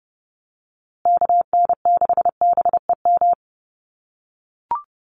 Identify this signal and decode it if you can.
Morse code, heard as a keyed tone, about 30 wpm, 700 Hz. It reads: KN6BEM
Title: USA Call Signs Repeated 3x 30wpm